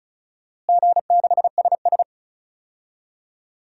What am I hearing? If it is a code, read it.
Morse code, heard as a keyed tone, about 36 wpm, 700 Hz. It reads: G6SS
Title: DXCC-100 Call Signs - 4 Characters - Repeated 3x - 35wpm